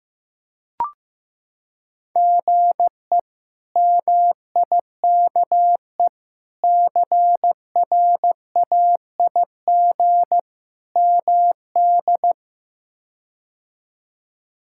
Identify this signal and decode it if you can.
Morse code, heard as a keyed tone, about 16 wpm, 700 Hz. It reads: GE MIKE CRAIG MD
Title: K1USN SST QSO Practice - 15wpm